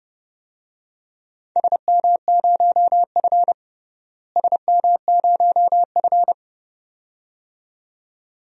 Morse code, heard as a keyed tone, about 31 wpm, 700 Hz. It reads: SM0F SM0F
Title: DXCC-100 Call Signs - 4 Characters - Repeated 2x - 30wpm